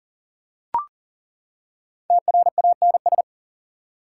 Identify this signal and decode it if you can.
Morse code, heard as a keyed tone, about 40 wpm, 700 Hz. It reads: TRANS
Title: Prefixes and Suffixes 40wpm